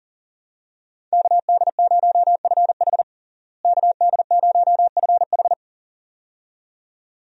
Morse code, heard as a keyed tone, about 40 wpm, 700 Hz. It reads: KD0FH KD0FH
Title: USA Call Signs - 5 Characters - Repeated 2x - 40wpm